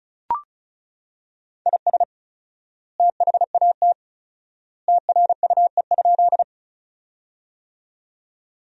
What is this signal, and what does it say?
Morse code, heard as a keyed tone, about 34 wpm, 700 Hz. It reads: IS THAT TRUE?
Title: 4x Word-Spacing - Sentences from Top 400 Words - 35wpm